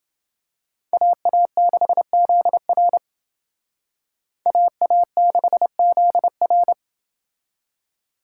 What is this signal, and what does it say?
Morse code, heard as a keyed tone, about 28 wpm, 700 Hz. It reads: AA6ZR AA6ZR